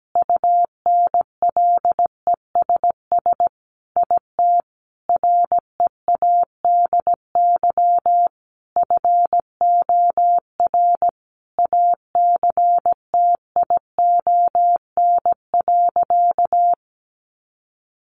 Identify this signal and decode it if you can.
Morse code, heard as a keyed tone, about 17 wpm, 700 Hz. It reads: UNLESS IT READY FOR ACTION.
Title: Sentences from Top 1300 Words 17wpm